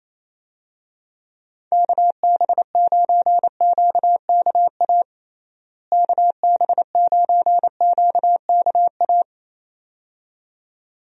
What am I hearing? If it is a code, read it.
Morse code, heard as a keyed tone, about 28 wpm, 700 Hz. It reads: KB9QKA KB9QKA
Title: USA Call Signs - 6 Characters - Repeated 2x - 28wpm